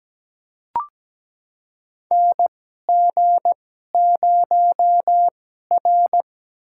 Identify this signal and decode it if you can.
Morse code, heard as a keyed tone, about 17 wpm, spaced as for 12 wpm, 700 Hz. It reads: NG0R
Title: USA Call Signs - 2x Character Spacing - 17wpm